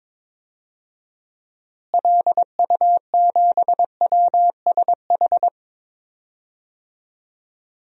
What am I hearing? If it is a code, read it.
Morse code, heard as a keyed tone, about 22 wpm, 700 Hz. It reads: LU7WSH
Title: DXCC-100 Call Signs - 6 Characters - Repeated 3x - 22wpm